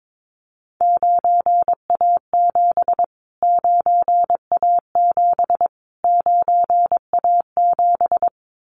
Morse code, heard as a keyed tone, about 22 wpm, 700 Hz. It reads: 9A7 9A7 9A7
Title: Callsign Prefixes for Top 25 DXCC - Repeated 3x - 22wpm